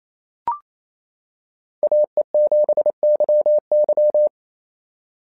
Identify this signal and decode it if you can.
Morse code, heard as a keyed tone, about 27 wpm, 590 Hz. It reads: AE7YY
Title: SOTA QSO Practice 28wpm